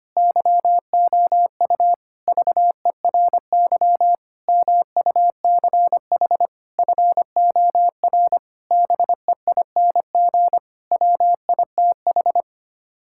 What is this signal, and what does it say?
Morse code, heard as a keyed tone, about 25 wpm, 700 Hz. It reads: YOU VERY MUCH FOR BEING WITH
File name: Sets of 9 Words - 25wpm